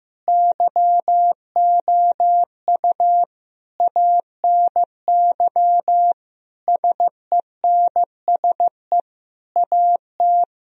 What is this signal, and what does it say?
Morse code, heard as a keyed tone, about 15 wpm, 700 Hz. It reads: YOU ANY SENSE AT
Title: Sentences from Top 1000 Words - Review Part 4 - 15wpm